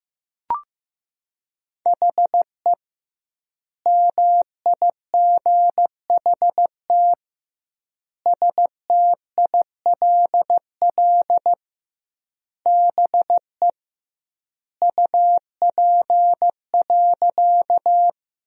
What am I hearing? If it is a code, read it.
Morse code, heard as a keyed tone, about 15 wpm, 700 Hz. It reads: HE MIGHT STILL BE UP.
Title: 2x Word-Spacing - Sentences from Top 300 Words - 15wpm